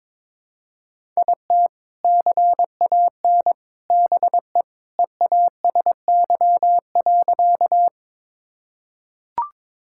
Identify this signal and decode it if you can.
Morse code, heard as a keyed tone, about 22 wpm, 700 Hz. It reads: IT CAN BE EASY.
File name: practice / 22wpm / Sentences with 2-Word Phrases - 22wpm